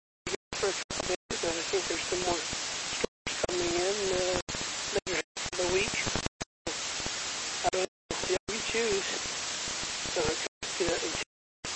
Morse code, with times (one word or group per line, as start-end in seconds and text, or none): none